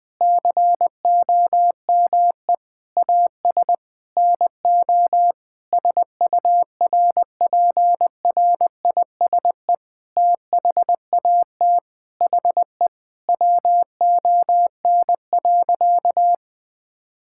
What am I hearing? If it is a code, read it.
Morse code, heard as a keyed tone, about 20 wpm, 700 Hz. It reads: COME AS NO SURPRISE THAT HE WON.
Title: Sentences with 6-Word Phrases - 20wpm